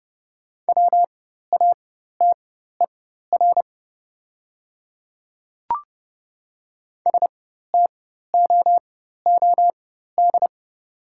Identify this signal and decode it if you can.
Morse code, heard as a keyed tone, about 30 wpm, spaced as for 14 wpm, 690 Hz. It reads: WATER STOOD
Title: 4x Character Spacing - Top 500 Words - 30wpm